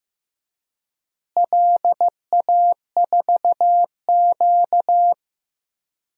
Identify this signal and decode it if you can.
Morse code, heard as a keyed tone, about 15 wpm, 700 Hz. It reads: LA4Q
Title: DXCC-100 Call Signs - 4 Characters - Repeated 2x - 15wpm